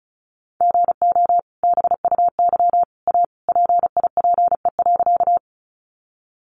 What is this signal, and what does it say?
Morse code, heard as a keyed tone, about 34 wpm, 700 Hz. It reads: GO BUY A PIPE.
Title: Sentences from Top 1500 Words 35wpm